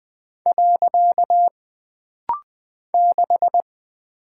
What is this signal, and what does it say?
Morse code, heard as a keyed tone, about 20 wpm, 700 Hz. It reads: . 6
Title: Single Character (Letter:Number:Punctuation:Prosign) - Rapid-Fire - 20wpm